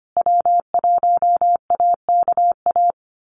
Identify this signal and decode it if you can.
Morse code, heard as a keyed tone, about 25 wpm, 700 Hz. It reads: W1AKA
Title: USA Call Signs Repeated 3x 25wpm